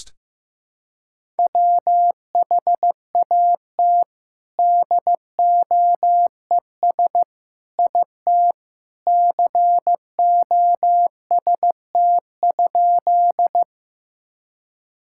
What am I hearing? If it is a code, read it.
Morse code, heard as a keyed tone, about 14 wpm, 700 Hz. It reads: WHAT DOES IT COST?
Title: Sentences from Top 1000 Words - Review Part 5 - 15wpm